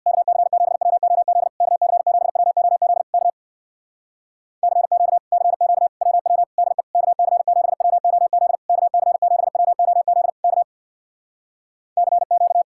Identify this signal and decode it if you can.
Morse code, heard as a keyed tone, about 64 wpm, 700 Hz. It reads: KC7WOZ KC7WOZ K CQ CQ WW DE KC7WOZ KC7WOZ K CQ